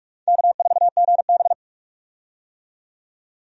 Morse code, heard as a keyed tone, about 46 wpm, 700 Hz. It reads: K4GB
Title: USA Call Signs - 4 Characters - 45wpm